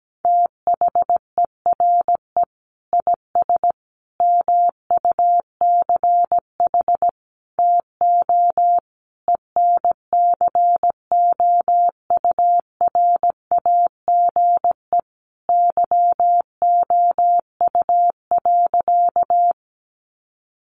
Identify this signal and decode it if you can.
Morse code, heard as a keyed tone, about 17 wpm, 700 Hz. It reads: THERE IS MUCH TO ENCOURAGE YOU.
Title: Sentences from Top 1400 Words 17wpm